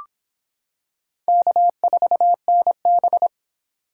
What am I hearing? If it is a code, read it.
Morse code, heard as a keyed tone, about 26 wpm, 700 Hz. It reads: K4NB